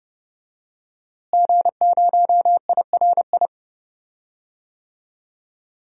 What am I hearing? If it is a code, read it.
Morse code, heard as a keyed tone, about 30 wpm, 700 Hz. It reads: G0IRI
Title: DXCC-100 Call Signs - 5 Characters - Repeated 3x - 30wpm